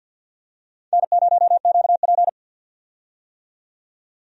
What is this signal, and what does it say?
Morse code, heard as a keyed tone, about 46 wpm, 700 Hz. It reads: N0QP
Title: USA Call Signs - 4 Characters - 50wpm